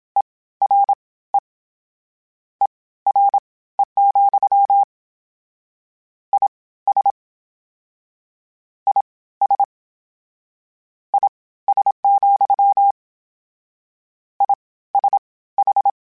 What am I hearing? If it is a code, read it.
Morse code, heard as a keyed tone, about 27 wpm, spaced as for 16 wpm, 800 Hz. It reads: ERE ERE, IS IS IS, ISH
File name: General / General-mp3s / Suffixes sp @25